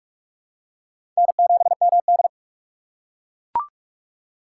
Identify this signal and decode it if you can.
Morse code, heard as a keyed tone, about 41 wpm, 700 Hz. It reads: N7MD
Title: USA Call Signs - 4 Characters - 4x Character Spacing - 45wpm